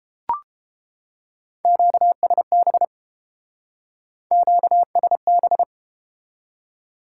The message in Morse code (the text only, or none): QSB QSB